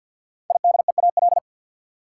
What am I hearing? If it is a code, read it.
Morse code, heard as a keyed tone, about 46 wpm, 700 Hz. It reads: IDEAL
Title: Sentences from Top 1400 Words - New Word Drill - 50wpm